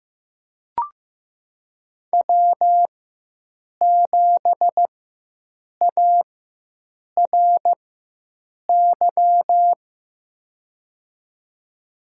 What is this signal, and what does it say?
Morse code, heard as a keyed tone, about 15 wpm, spaced as for 7 wpm, 700 Hz. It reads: W7ARY
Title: USA Call Signs - 4x Character Spacing - 15wpm